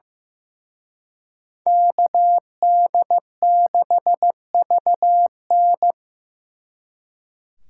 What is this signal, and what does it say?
Morse code, heard as a keyed tone, about 18 wpm, 700 Hz. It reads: KD6VN
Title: Speed Racing - USA Call Signs 15wpm